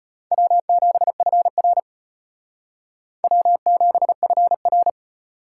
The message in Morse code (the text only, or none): W7FR W7FR